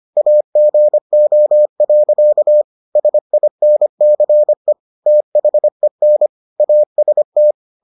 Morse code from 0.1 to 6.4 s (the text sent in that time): AGO. SINCE THEN